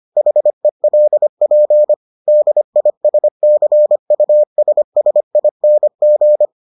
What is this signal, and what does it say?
Morse code, heard as a keyed tone, about 25 wpm, 600 Hz. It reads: HELP DISCUSSING